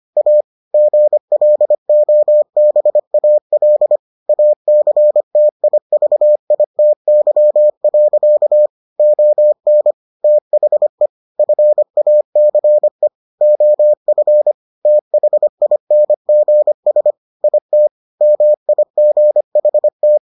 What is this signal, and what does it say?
Morse code, heard as a keyed tone, about 25 wpm, 600 Hz. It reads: A GLOBAL ACTIVITY. ON THE FACE OF THINGS IT MIGHT